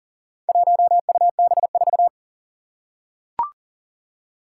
Morse code, heard as a keyed tone, about 38 wpm, 700 Hz. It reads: 1UB4
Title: Four Challenging Letter-Number Combinations 40wpm